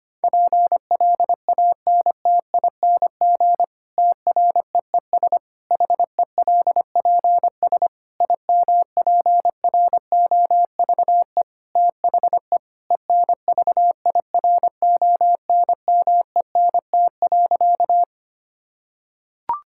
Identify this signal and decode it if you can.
Morse code, heard as a keyed tone, about 25 wpm, 700 Hz. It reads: PLANTING TREES HELPS IMPROVE THE ENVIRONMENT.
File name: practice / 25wpm / Difficult Sentences with Hints - 25wpm